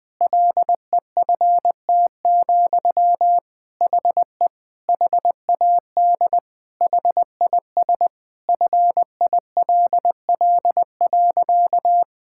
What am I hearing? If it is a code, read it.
Morse code, heard as a keyed tone, about 19 wpm, 700 Hz. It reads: LEFT, HE HAD HIS FILL.
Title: Sentences from Top 1000 Words - Review Part 5 - 20wpm